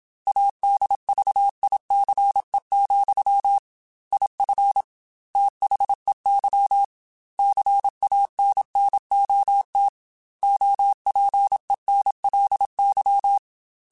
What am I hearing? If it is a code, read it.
Morse code, heard as a keyed tone, about 26 wpm, 800 Hz. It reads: ADVICE, IF THEY CANNOT OPENLY
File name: Quotes / Quotes-mp3s / Advice @25